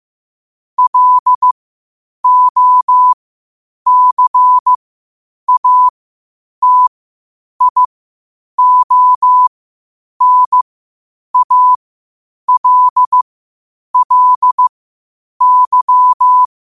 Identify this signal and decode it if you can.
Morse code, heard as a keyed tone, about 15 wpm, spaced as for 8 wpm, 1000 Hz. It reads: LOCATIONALLY
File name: qotdfiles / 8wpm